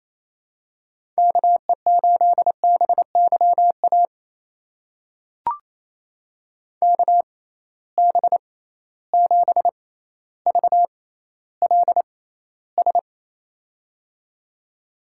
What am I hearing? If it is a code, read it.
Morse code, heard as a keyed tone, about 28 wpm, spaced as for 14 wpm, 700 Hz. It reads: KE8BYA KB7VLS
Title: USA Call Signs - 6 Characters - 6x Character Spacing - 28wpm